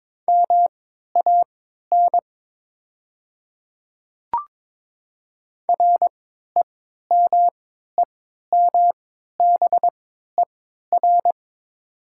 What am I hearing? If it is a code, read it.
Morse code, heard as a keyed tone, about 22 wpm, spaced as for 12 wpm, 700 Hz. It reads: MAN REMEMBER
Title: 3x Character Spacing - Top 500 Words - 22wpm